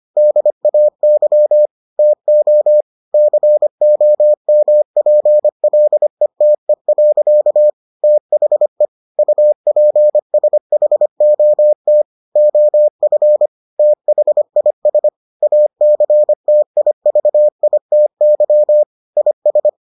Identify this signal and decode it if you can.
Morse code, heard as a keyed tone, about 25 wpm, 600 Hz. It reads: DAY TO COMPLETE. THE UPSHOT OF THIS ACTIVITY IS